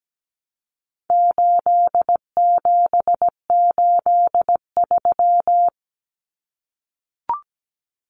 Morse code, heard as a keyed tone, about 17 wpm, 700 Hz. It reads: 8783
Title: Numbers - 4-Digit 17wpm